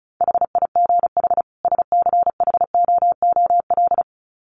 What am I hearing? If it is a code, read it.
Morse code, heard as a keyed tone, about 34 wpm, 700 Hz. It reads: HIGH SCHOOL
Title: Sets of 2 Words 35wpm